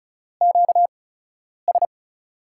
Morse code, heard as a keyed tone, about 36 wpm, spaced as for 10 wpm, 700 Hz. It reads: QS
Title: USA Call Signs - 8x Character Spacing - 35wpm